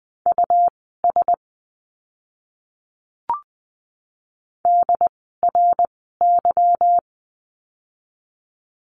Morse code, heard as a keyed tone, about 20 wpm, spaced as for 14 wpm, 700 Hz. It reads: US DRY
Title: 146 Lesson 24f - Letter y - Words 20wpm